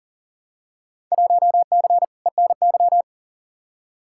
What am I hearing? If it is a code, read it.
Morse code, heard as a keyed tone, about 40 wpm, 700 Hz. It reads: 1C ENY
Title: Field Day Exchanges Encore - 40wpm